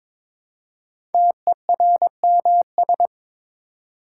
Morse code, heard as a keyed tone, about 22 wpm, 700 Hz. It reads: TERMS
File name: 5 Letter Words - First 3 Letters Given - 22wpm